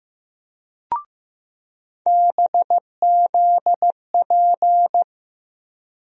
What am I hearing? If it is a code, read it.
Morse code, heard as a keyed tone, about 15 wpm, 700 Hz. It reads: BZP
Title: USA Call Sign Postfixes - 3 Letters 15wpm